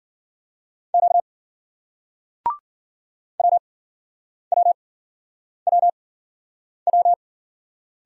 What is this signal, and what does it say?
Morse code, heard as a keyed tone, about 56 wpm, 700 Hz. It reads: Q W W W W